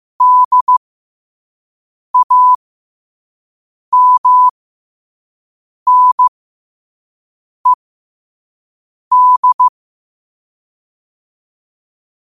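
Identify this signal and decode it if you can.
Morse code, heard as a keyed tone, about 15 wpm, spaced as for 5 wpm, 1000 Hz. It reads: DAMNED
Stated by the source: Morse code quote of the day at 5 wpm.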